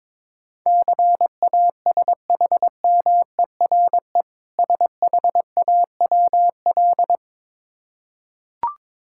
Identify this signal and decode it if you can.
Morse code, heard as a keyed tone, about 22 wpm, 690 Hz. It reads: CASHMERE SHAWL
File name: Difficult Words with Hints - 22wpm